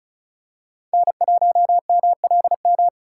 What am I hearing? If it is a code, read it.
Morse code, heard as a keyed tone, about 34 wpm, 700 Hz. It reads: N1MLM
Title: USA Call Signs - 3x Character Spacing - 35wpm